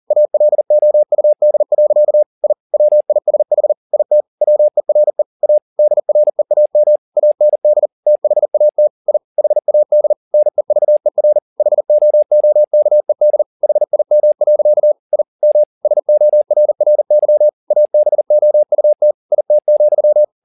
Morse code, heard as a keyed tone, about 40 wpm, 600 Hz. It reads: ALOUD. I WISH IT WERE A DREAM AND THAT I HAD NEVER HOOKED HIM. I M SORRY ABOUT IT,
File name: the old man and the sea 2 40wpm